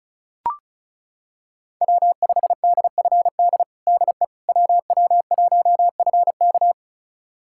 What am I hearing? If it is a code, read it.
Morse code, heard as a keyed tone, about 36 wpm, 700 Hz. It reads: W5DFD DE WW1FK
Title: USA Callsign DE Callsign - 5 Characters - 35wpm